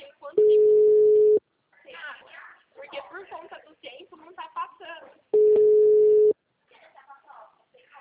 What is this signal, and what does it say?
Ringback tone (Europe, 425 Hz), with speech.